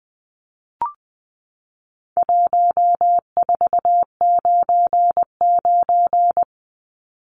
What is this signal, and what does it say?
Morse code, heard as a keyed tone, about 20 wpm, 700 Hz. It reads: 1499
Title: Numbers - 4-Digit 20wpm